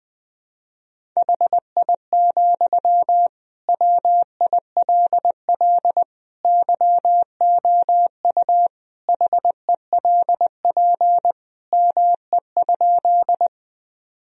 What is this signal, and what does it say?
Morse code, heard as a keyed tone, about 20 wpm, 700 Hz. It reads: HI, WILL YOU HELP ME?
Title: Sentences from Top 1000 Words - Review Part 3 - 20wpm